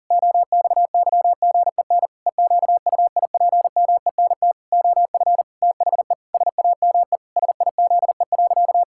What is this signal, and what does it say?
Morse code, heard as a keyed tone, about 40 wpm, 700 Hz. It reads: OXYGEN EQUIPMENT OF THE SAME SIZE.
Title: Pangram Challenge v2 - 40wpm